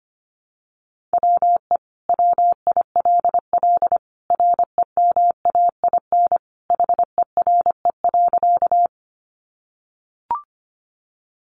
Morse code, heard as a keyed tone, about 25 wpm, 700 Hz. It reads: WE WILL REMAIN HERE.
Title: Sentences from Top 1200 Words 25wpm